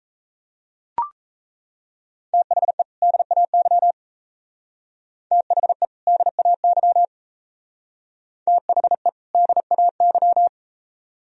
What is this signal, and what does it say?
Morse code, heard as a keyed tone, about 35 wpm, 700 Hz. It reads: THE DAY THE DAY THE DAY
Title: Speed-Racing - Sets of 2 Words 28wpm